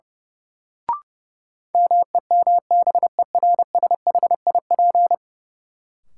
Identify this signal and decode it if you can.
Morse code, heard as a keyed tone, about 32 wpm, 700 Hz. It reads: MEMBERSHIP
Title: Sentences from Top 1400 Words - New Word Drill Alt - 30wpm